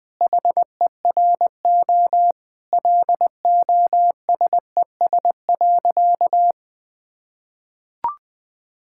Morse code, heard as a keyed tone, about 19 wpm, 700 Hz. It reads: HERO LOSES.
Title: Sentences with 6-Word Phrases - 20wpm